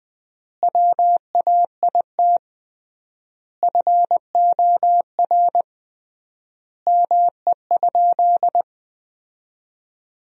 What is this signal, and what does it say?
Morse code, heard as a keyed tone, about 20 wpm, 700 Hz. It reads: WAIT FOR ME?
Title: 3x Word-Spacing - Sentences from Top 500 Words - 20wpm